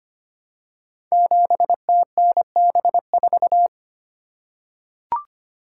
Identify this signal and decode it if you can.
Morse code, heard as a keyed tone, about 25 wpm, 700 Hz. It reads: 7TNB4
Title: Code Groups - 5 Letter:Numbers - 25wpm